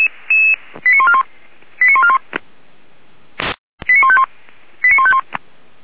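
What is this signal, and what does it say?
Five-tone selective call (ZVEI), heard as quick runs of five tones, tones of about 70 ms. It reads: 89141 89141 89151 89151
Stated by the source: ZVEI five-tone selective call received on an SDR.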